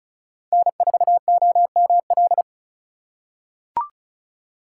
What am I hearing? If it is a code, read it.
Morse code, heard as a keyed tone, about 36 wpm, 700 Hz. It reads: N4OML
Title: USA Call Signs - 4x Character Spacing - 35wpm